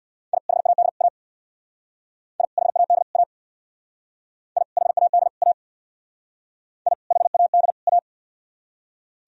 Morse code, heard as a keyed tone, about 61 wpm, 700 Hz. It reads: I HAD A I HAD A I HAD A I HAD A
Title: Speed-Racing - Sets of 3 Words 50wpm